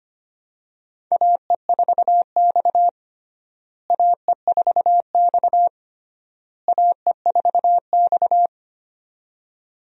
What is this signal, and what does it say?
Morse code, heard as a keyed tone, about 25 wpm, 700 Hz. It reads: AE4X AE4X AE4X
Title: USA Call Signs - 4 Characters - Repeated 3x - 25wpm